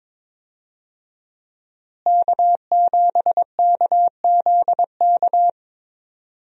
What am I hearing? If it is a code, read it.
Morse code, heard as a keyed tone, about 22 wpm, 700 Hz. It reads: K7KZK
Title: USA Call Signs - 5 Characters - Repeated 2x - 22wpm